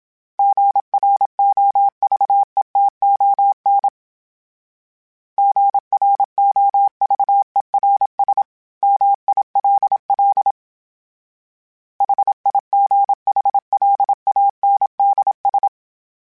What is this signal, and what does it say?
Morse code, heard as a keyed tone, about 27 wpm, 800 Hz. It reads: GROVETON GROVERS MILL HIGHLANDS
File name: fromSMs / FromSMs mp3s / Cities@25